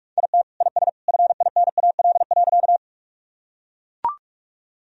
Morse code, heard as a keyed tone, about 41 wpm, 700 Hz. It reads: IT IS FINAL.
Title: Sentences from Top 1000 Words - Review Part 2 - 45wpm